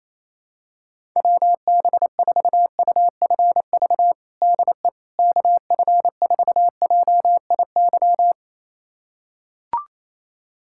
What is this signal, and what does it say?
Morse code, heard as a keyed tone, about 27 wpm, 700 Hz. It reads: WB4UFV DE KF4JIY
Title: USA Callsign DE Callsign - 6 Characters - 28wpm